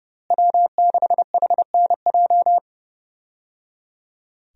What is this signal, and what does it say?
Morse code, heard as a keyed tone, about 30 wpm, 700 Hz. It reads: W6HNJ
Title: USA Call Signs - 5 Characters - Repeated 3x - 30wpm